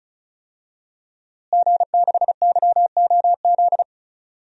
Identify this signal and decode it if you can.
Morse code, heard as a keyed tone, about 34 wpm, 700 Hz. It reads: G6YOZ
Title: DXCC-100 Call Signs - 5 Characters - Repeated 2x - 35wpm